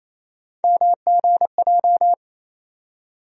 Morse code, heard as a keyed tone, about 27 wpm, 700 Hz. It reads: MGJ
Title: USA Call Sign Postfixes - 3 Letters 28wpm